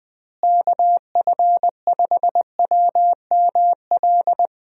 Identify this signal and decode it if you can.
Morse code, heard as a keyed tone, about 20 wpm, 700 Hz. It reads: KF5WML
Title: USA Call Signs - 6 Characters - Repeated 3x - 20wpm